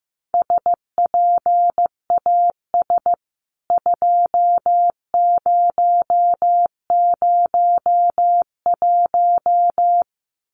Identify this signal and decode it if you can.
Morse code, heard as a keyed tone, about 15 wpm, 700 Hz. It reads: SPAS 2001
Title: CWT Exchange 15wpm